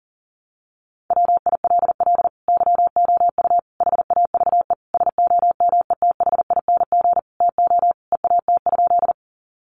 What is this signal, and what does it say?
Morse code, heard as a keyed tone, about 40 wpm, 700 Hz. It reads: WILL YOU HAVE SOMETHING TO EAT?
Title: Sentences from Top 600 Words 40wpm